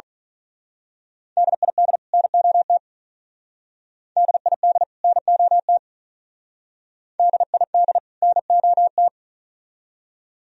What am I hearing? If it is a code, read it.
Morse code, heard as a keyed tone, about 39 wpm, 700 Hz. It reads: DID NOT DID NOT DID NOT
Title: Speed-Racing - Sets of 2 Words 35wpm